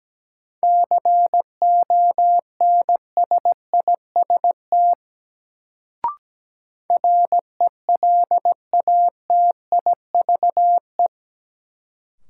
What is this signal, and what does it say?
Morse code, heard as a keyed tone, about 17 wpm, 700 Hz. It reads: CONSIST RELATIVE
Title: Sentences from Top 1400 Words - New Word Drill - 17wpm